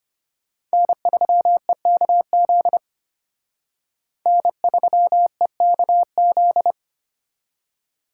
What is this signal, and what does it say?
Morse code, heard as a keyed tone, about 32 wpm, 700 Hz. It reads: N3EKZ N3EKZ